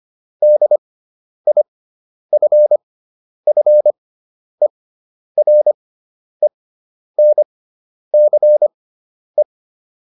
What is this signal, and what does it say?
Morse code, heard as a keyed tone, about 25 wpm, spaced as for 10 wpm, 600 Hz. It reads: DIFFERENCE